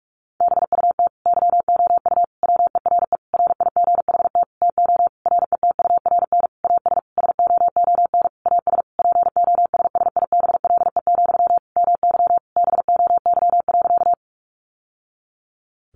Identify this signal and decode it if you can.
Morse code, heard as a keyed tone, about 44 wpm, 700 Hz. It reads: BUT YOU WERE RIGHT TO RETURN AS SOON AS POSSIBLE, MY BOY.
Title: Sentences from Top 800 Words 45wpm